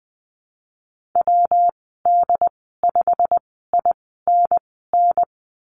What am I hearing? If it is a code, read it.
Morse code, heard as a keyed tone, about 20 wpm, spaced as for 15 wpm, 700 Hz. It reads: WD5INN
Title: 128 Lesson 21g - Number 6 - Call Signs 20wpm